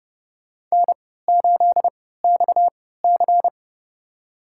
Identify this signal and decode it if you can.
Morse code, heard as a keyed tone, about 30 wpm, spaced as for 17 wpm, 700 Hz. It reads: N8XC
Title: USA Call Signs - 4 Characters - 3x Character Spacing - 30wpm